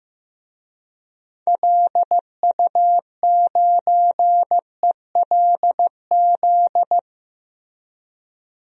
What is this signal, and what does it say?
Morse code, heard as a keyed tone, about 16 wpm, 700 Hz. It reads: LU9ELZ
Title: DXCC-100 Call Signs - 6 Characters - 15wpm